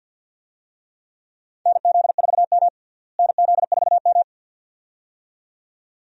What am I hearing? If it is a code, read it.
Morse code, heard as a keyed tone, about 52 wpm, 700 Hz. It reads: NZ4M NZ4M
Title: USA Call Signs - 4 Characters - Repeated 2x - 50wpm